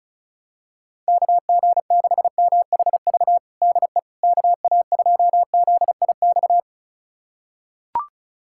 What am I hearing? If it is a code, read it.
Morse code, heard as a keyed tone, about 34 wpm, 700 Hz. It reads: KG6MHV DE KA2ZIX
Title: USA Callsign DE Callsign 35wpm